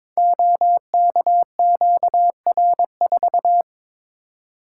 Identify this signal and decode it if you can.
Morse code, heard as a keyed tone, about 22 wpm, 690 Hz. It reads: OKQR4
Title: Code Groups - 5 Letter:Numbers - 22wpm